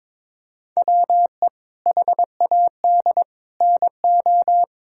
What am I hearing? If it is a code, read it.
Morse code, heard as a keyed tone, about 22 wpm, 700 Hz. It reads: WE HAD NO
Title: Sentences from Top 1000 Words - Review Part 6 - 22wpm